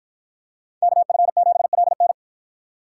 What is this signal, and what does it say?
Morse code, heard as a keyed tone, about 49 wpm, 700 Hz. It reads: KF7LN